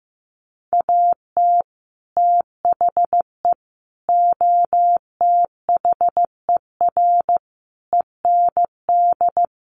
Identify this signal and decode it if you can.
Morse code, heard as a keyed tone, about 15 wpm, 700 Hz. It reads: AT THE OTHER END
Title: Sets of 5 Words 15wpm